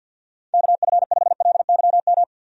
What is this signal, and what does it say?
Morse code, heard as a keyed tone, about 51 wpm, 700 Hz. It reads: KF5LYM
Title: USA Call Signs - 6 Characters - 50wpm